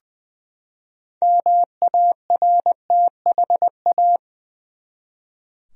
Morse code, heard as a keyed tone, about 20 wpm, 700 Hz. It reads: MARTHA
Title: USA Common Names - 20wpm